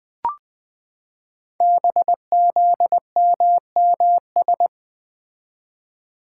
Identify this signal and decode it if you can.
Morse code, heard as a keyed tone, about 20 wpm, 700 Hz. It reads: BZMMS
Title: Code Groups - 5 Letters - 20wpm